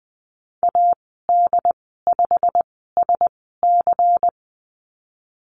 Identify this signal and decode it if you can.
Morse code, heard as a keyed tone, about 20 wpm, spaced as for 15 wpm, 700 Hz. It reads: AD5SC
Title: 147 Lesson 24g - Letter y - Call Signs 20wpm